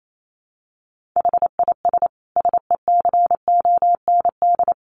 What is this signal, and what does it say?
Morse code, heard as a keyed tone, about 28 wpm, 700 Hz. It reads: HIS SECOND
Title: Sentences from Top 1400 Words 28wpm